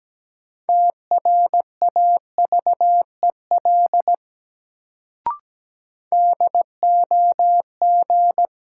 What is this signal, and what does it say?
Morse code, heard as a keyed tone, about 17 wpm, 700 Hz. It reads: TRAVEL DOG
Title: Sentences from Top 400 Words - New Word Drill - 17wpm